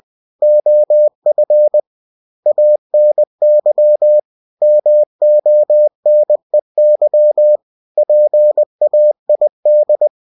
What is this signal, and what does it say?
Morse code, heard as a keyed tone, about 20 wpm, 600 Hz. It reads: OF ANY MONEY PAID
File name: cw-books / winnie-the-pooh / winnie-20wpm-page-0117